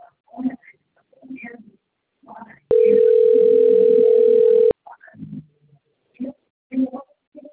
Ringback tone (North America), 440 + 480 Hz.
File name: rg-1600-unknown- - - .6362